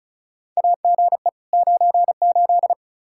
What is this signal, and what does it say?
Morse code, heard as a keyed tone, about 36 wpm, 700 Hz. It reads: AGE 98
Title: Practice QSO - Exchange 3 - More Info - 35wpm